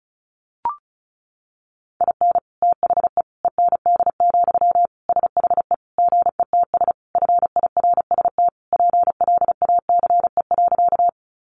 Morse code, heard as a keyed tone, about 34 wpm, 700 Hz. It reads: IN THE END, SHE GETS FIRST PLACE.
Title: Additional Easy Phrases 35wpm